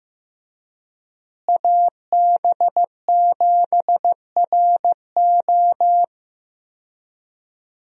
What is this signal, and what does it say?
Morse code, heard as a keyed tone, about 15 wpm, 700 Hz. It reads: AB7RO
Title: USA Call Signs - 5 Characters - Repeated 3x - 15wpm